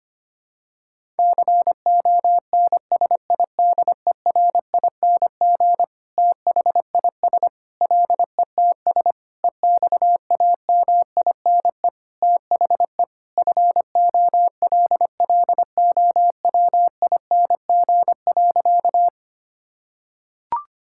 Morse code, heard as a keyed tone, about 25 wpm, 700 Hz. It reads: CONSIDERING THIS LETS EXAMINE THE FOLLOWING.
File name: practice / 25wpm / Sentences with 9-Word Phrases - 25wpm